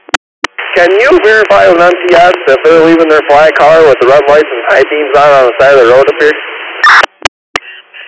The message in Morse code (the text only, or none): none